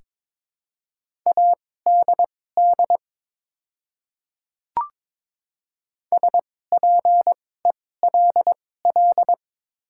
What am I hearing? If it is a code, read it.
Morse code, heard as a keyed tone, about 22 wpm, spaced as for 16 wpm, 700 Hz. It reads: ADD SPELL
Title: 2x Character Spacing - Top 200 Words - 22wpm